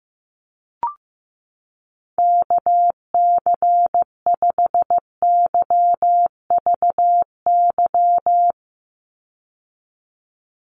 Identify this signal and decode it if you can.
Morse code, heard as a keyed tone, about 15 wpm, 700 Hz. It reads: KC5YVY
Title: USA Call Signs 15wpm